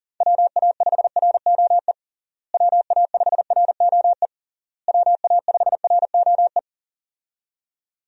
Morse code, heard as a keyed tone, about 40 wpm, 700 Hz. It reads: WA5ROE WA5ROE WA5ROE
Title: USA Call Signs - 6 Characters - Repeated 3x - 40wpm